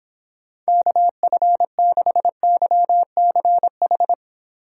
Morse code, heard as a keyed tone, about 26 wpm, 700 Hz. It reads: KF6YCH